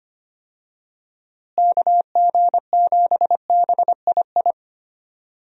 Morse code, heard as a keyed tone, about 25 wpm, 700 Hz. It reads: KG7BII
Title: USA Call Signs - 6 Characters - Repeated 2x - 25wpm